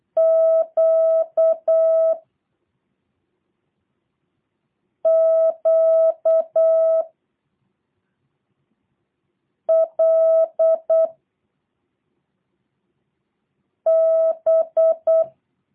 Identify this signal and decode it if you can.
Morse code, heard as a keyed tone, about 8 wpm, 650 Hz. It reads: Q Q L B
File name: jh8cbh / 2023cw / review1-6